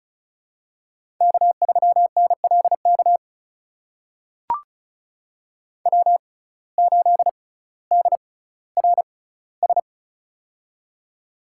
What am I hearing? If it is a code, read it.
Morse code, heard as a keyed tone, about 34 wpm, spaced as for 18 wpm, 700 Hz. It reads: K3NLK W8DRS
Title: USA Call Signs - 5 Characters - 6x Character Spacing - 35wpm